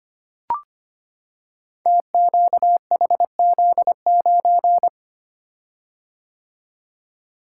Morse code, heard as a keyed tone, about 25 wpm, 700 Hz. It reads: TQHZ9
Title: Code Groups - 5 Letter:Numbers - 25wpm